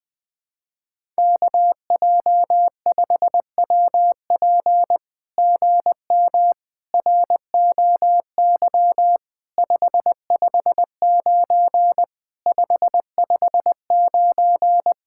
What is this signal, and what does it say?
Morse code, heard as a keyed tone, about 20 wpm, 700 Hz. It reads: KJ5WP GM ROY 559 559
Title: SOTA QSO Practice 20wpm